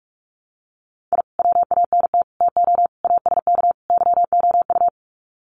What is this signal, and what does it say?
Morse code, heard as a keyed tone, about 44 wpm, 700 Hz. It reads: I WANT TO ASK YOU
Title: Sets of 5 Words 45wpm